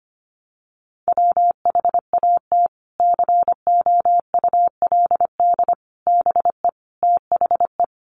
Morse code, heard as a keyed tone, about 25 wpm, 700 Hz. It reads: WHAT COULD BE THE
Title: Sentences from Top 900 Words 25wpm